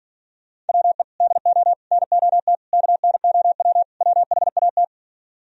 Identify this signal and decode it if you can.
Morse code, heard as a keyed tone, about 46 wpm, 700 Hz. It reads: WE DO NOT KNOW WHAT